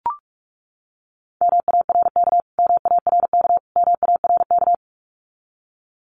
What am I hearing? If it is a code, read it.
Morse code, heard as a keyed tone, about 44 wpm, 700 Hz. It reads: MARK MARK MARK
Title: Top 300 Words Repeated 45wpm